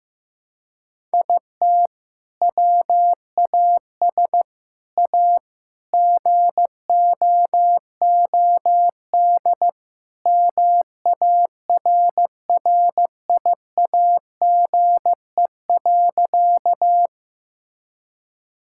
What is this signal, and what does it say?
Morse code, heard as a keyed tone, about 15 wpm, 700 Hz. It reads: IT WAS A GOOD MARRIAGE.
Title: Sentences from Top 1000 Words - Review Part 5 - 15wpm